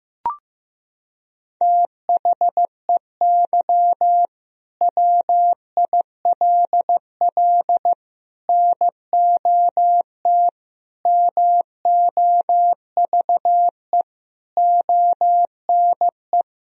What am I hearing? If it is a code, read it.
Morse code, heard as a keyed tone, about 16 wpm, 700 Hz. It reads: THEY WILL NOT MOVE ONE
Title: Sentences from Top 1000 Words - Review Part 3 - 15wpm